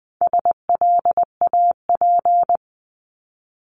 Morse code, heard as a keyed tone, about 20 wpm, 700 Hz. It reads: SLAP
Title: 260 Lesson 42e - Words 20-20wpm